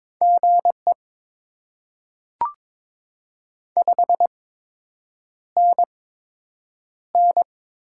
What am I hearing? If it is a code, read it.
Morse code, heard as a keyed tone, about 22 wpm, spaced as for 8 wpm, 700 Hz. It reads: GE 5NN
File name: Top 20 QSO Elements - 8x Character Spacing - 22wpm